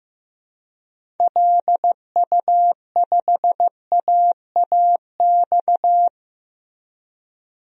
Morse code, heard as a keyed tone, about 15 wpm, 700 Hz. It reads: LU5AAX
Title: DXCC-100 Call Signs - 6 Characters - Repeated 3x - 15wpm